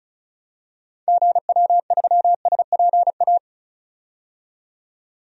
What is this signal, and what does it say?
Morse code, heard as a keyed tone, about 34 wpm, 700 Hz. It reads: GW3SPA
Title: DXCC-100 Call Signs - 6 Characters - 35wpm